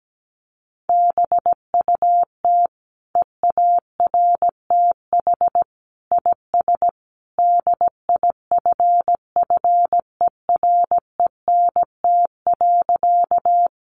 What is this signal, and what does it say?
Morse code, heard as a keyed tone, about 17 wpm, 700 Hz. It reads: BUT EARTH IS DIFFERENT.
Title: Sentences from Top 600 Words 17wpm